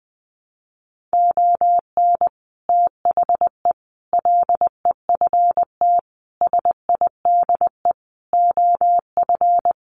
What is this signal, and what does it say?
Morse code, heard as a keyed tone, about 20 wpm, 700 Hz. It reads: ON THE LEFT SIDE OF
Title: Sets of 5 Words Encore2 20wpm